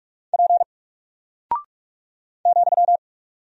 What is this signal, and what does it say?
Morse code, heard as a keyed tone, about 44 wpm, 700 Hz. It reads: P ,
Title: Single Character (Letter:Number:Punctuation:Prosign) 45wpm